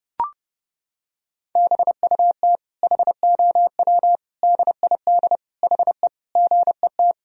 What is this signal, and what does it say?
Morse code, heard as a keyed tone, about 30 wpm, 700 Hz. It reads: BUT HOW DID HE GET
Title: Sentences from Top 1000 Words - Review Part 4 - 30wpm